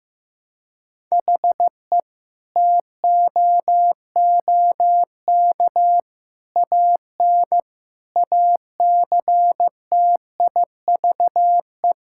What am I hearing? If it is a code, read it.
Morse code, heard as a keyed tone, about 14 wpm, 700 Hz. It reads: HE TOOK AN ACTIVE
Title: Sentences from Top 1000 Words - Review Part 6 - 15wpm